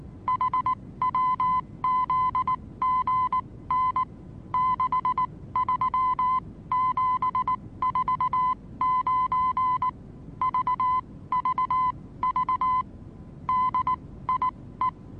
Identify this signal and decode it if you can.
Morse code, heard as a keyed tone, about 19 wpm, spaced as for 15 wpm, 1020 Hz. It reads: HWZGN 63749 VVV DIE
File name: repos / Morsekurs / DJ4UF Morsen26